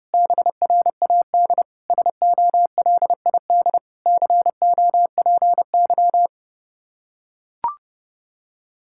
Morse code, heard as a keyed tone, about 32 wpm, 700 Hz. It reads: BRAD SOLID COPY
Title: Practice QSO - Exchange 3 - More Info - 30wpm